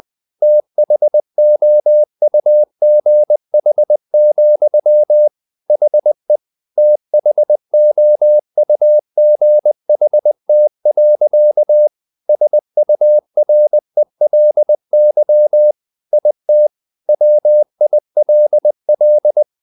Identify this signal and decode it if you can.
Morse code, heard as a keyed tone, about 20 wpm, 600 Hz. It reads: THOUGH, HE THOUGHT. SURELY IT WILL
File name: the old man and the sea / the old man and the sea part 1 20wpm